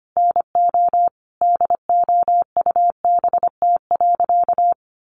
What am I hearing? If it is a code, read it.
Morse code, heard as a keyed tone, about 25 wpm, 700 Hz. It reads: NO DOUBT.
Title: Sentences from Top 1500 Words 25wpm